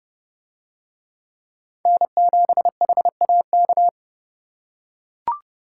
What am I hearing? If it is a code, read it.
Morse code, heard as a keyed tone, about 30 wpm, 700 Hz. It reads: N7HAK